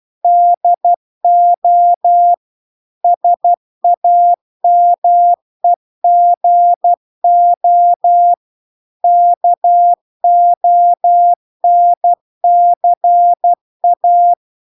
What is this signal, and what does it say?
Morse code, heard as a keyed tone, about 12 wpm, 700 Hz. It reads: DO SAMEGO KONCA